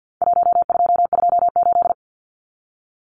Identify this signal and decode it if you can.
Morse code, heard as a keyed tone, about 49 wpm, 700 Hz. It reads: 1228
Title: Numbers - Contest Style - 4-Digit 50wpm